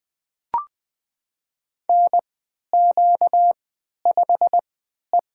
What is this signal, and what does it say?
Morse code, heard as a keyed tone, about 20 wpm, spaced as for 11 wpm, 700 Hz. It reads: NQ5E
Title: USA Call Signs - 3x Character Spacing - 20wpm